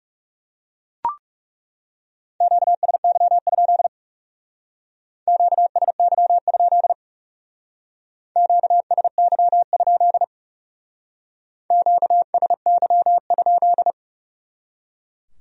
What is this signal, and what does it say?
Morse code, heard as a keyed tone, about 37 wpm, 700 Hz. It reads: QSY? QSY? QSY? QSY?
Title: Speed-Racing Q-Codes - 30wpm